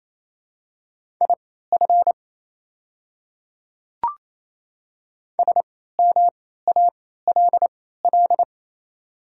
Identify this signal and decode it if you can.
Morse code, heard as a keyed tone, about 28 wpm, spaced as for 16 wpm, 700 Hz. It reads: IF SMALL
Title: 3x Character Spacing - Top 200 Words - 28wpm